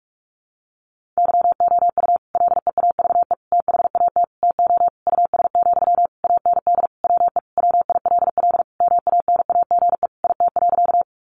Morse code, heard as a keyed tone, about 44 wpm, 700 Hz. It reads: YOU LEAVE THAT TO US, AND WE WILL MANAGE IT.
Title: Sentences from Top 1400 Words 45wpm